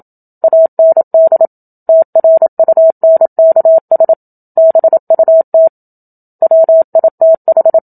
Morse code, heard as a keyed tone, about 27 wpm, 650 Hz. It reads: AND TRUNKS BUT WITH
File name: Book Two / 01-Chapter 1 - 27 WPM